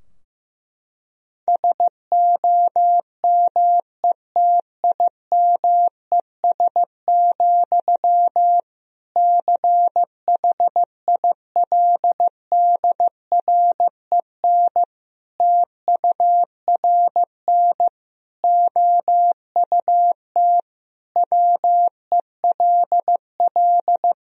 Morse code, heard as a keyed tone, about 15 wpm, 700 Hz. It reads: SOMETIMES, CHILDREN TURN OUT WELL
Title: Sentences from Top 1000 Words - Review Part 3 - 15wpm